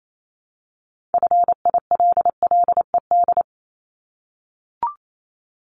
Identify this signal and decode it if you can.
Morse code, heard as a keyed tone, about 28 wpm, 700 Hz. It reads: FILLED
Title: Multi-letter Phonemes Repeated 3x 28wpm